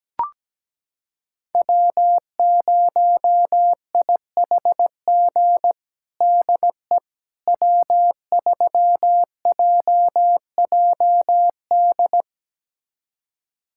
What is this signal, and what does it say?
Morse code, heard as a keyed tone, about 17 wpm, 700 Hz. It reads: W0IHG DE W3JJD
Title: USA Callsign DE Callsign - 5 Characters - 17wpm